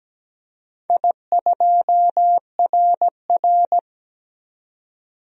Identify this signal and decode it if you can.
Morse code, heard as a keyed tone, about 17 wpm, 700 Hz. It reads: I2RR
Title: DXCC-100 Call Signs - 4 Characters - 17wpm